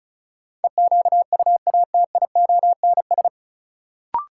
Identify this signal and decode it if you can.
Morse code, heard as a keyed tone, about 36 wpm, 700 Hz. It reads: EQUATIONS
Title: Sets of 5 Words - New Word Drill - 35wpm